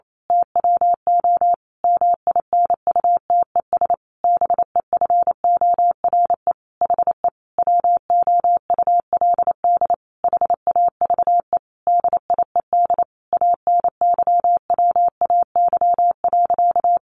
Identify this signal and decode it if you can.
Morse code, heard as a keyed tone, about 28 wpm, 700 Hz. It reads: TWO MINUTES BEFORE HE WOULD HAVE DIED ANYWAY.
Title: A Sound Of Thunder 28wpm